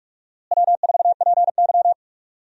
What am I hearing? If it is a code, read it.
Morse code, heard as a keyed tone, about 45 wpm, 700 Hz. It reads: W4PY
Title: USA Call Signs - 4 Characters - Repeated 3x - 45wpm